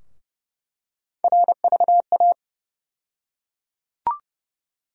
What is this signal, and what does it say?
Morse code, heard as a keyed tone, about 30 wpm, 700 Hz. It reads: RVA
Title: USA Call Sign Postfixes - 3 Letters 30wpm